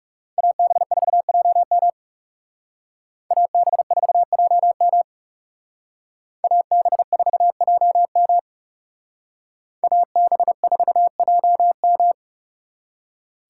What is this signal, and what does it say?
Morse code, heard as a keyed tone, about 35 wpm, 700 Hz. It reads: AB4JM AB4JM AB4JM AB4JM
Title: Speed Racing - USA 2x2 Call Signs 30wpm